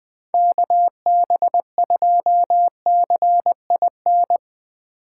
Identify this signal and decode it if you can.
Morse code, heard as a keyed tone, about 22 wpm, 700 Hz. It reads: KB2CIN
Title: Speed Racing - USA 2x3 Call Signs 20wpm